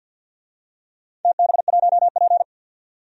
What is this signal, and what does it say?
Morse code, heard as a keyed tone, about 51 wpm, 700 Hz. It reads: TB1P
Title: DXCC-100 Call Signs - 4 Characters - 50wpm